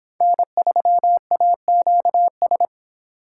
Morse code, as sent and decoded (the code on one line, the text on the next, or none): -. ...-- .- --.- ...
N3AQS